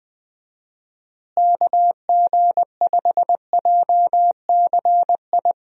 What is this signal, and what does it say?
Morse code, heard as a keyed tone, about 20 wpm, 700 Hz. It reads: KG5JCI
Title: USA Call Signs - 6 Characters - Repeated 3x - 20wpm